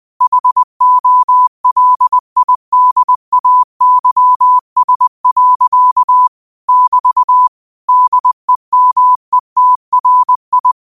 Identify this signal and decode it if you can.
Morse code, heard as a keyed tone, about 20 wpm, 1000 Hz. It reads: HOLIDAYS. <BT> DEMETRI
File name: qotdfiles / 20wpm